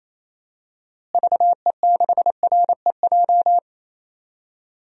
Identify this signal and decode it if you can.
Morse code, heard as a keyed tone, about 28 wpm, 700 Hz. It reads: VE6REJ
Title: DXCC-100 Call Signs - 6 Characters - Repeated 3x - 28wpm